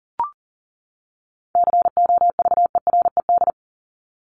Morse code, heard as a keyed tone, about 40 wpm, 700 Hz. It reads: COVERED
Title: ED Suffix - Top 100 Words with Base Words 40wpm